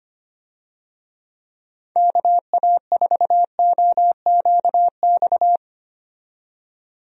Morse code, heard as a keyed tone, about 25 wpm, 700 Hz. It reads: KA4OQX
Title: USA Call Signs - 6 Characters - 25wpm